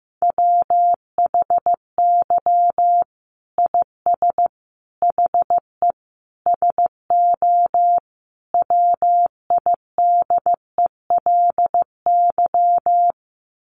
Morse code, heard as a keyed tone, about 15 wpm, 700 Hz. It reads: WHY IS HE SO WIDELY
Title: Sentences from Top 1200 Words 15wpm